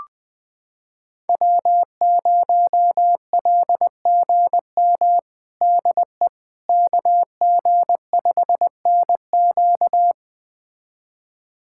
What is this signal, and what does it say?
Morse code, heard as a keyed tone, about 21 wpm, 700 Hz. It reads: W0LGM DE KG5NQ
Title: USA Callsign DE Callsign - 5 Characters - 20wpm